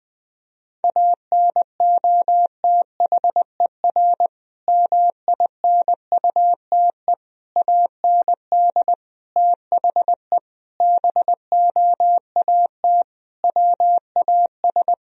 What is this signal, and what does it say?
Morse code, heard as a keyed tone, about 20 wpm, 700 Hz. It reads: ANOTHER MINUTE AND THE BOAT WAS
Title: Sentences from Top 1000 Words - Review Part 3 - 20wpm